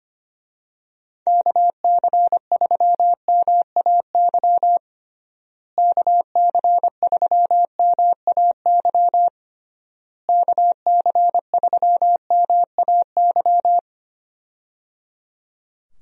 Morse code, heard as a keyed tone, about 25 wpm, 700 Hz. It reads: KC3MAY KC3MAY KC3MAY
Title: USA Call Signs - 6 Characters - Repeated 3x - 25wpm